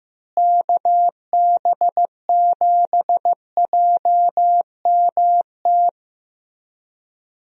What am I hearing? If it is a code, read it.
Morse code, heard as a keyed tone, about 15 wpm, 700 Hz. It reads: KB7JMT
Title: USA Call Signs - 6 Characters - Repeated 3x - 15wpm